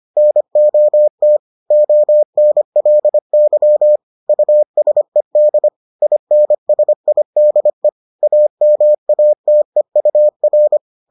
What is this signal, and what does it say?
Morse code, heard as a keyed tone, about 25 wpm, 600 Hz. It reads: NOT ONLY USED INSIDE AMATEUR